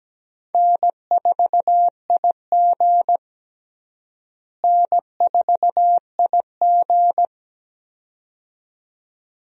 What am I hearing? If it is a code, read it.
Morse code, heard as a keyed tone, about 17 wpm, 700 Hz. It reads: N4IG N4IG
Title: USA Call Signs - 4 Characters - Repeated 2x - 17wpm